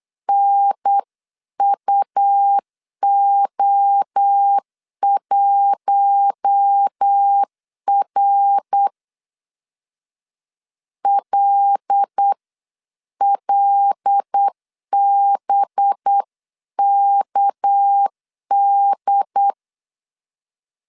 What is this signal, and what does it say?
Morse code, heard as a keyed tone, about 8 wpm, 800 Hz. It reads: NUO1R LLBKD
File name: CW / DET / 1 6 1500C S B 8wpm 5142